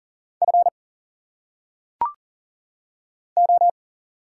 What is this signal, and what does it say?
Morse code, heard as a keyed tone, about 40 wpm, 700 Hz. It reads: F O
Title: Single Letters - Phonetic Pronunciation 40wpm